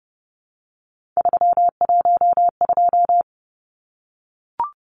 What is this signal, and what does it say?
Morse code, heard as a keyed tone, about 30 wpm, 700 Hz. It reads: 312
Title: Numbers - 3-Digit 30wpm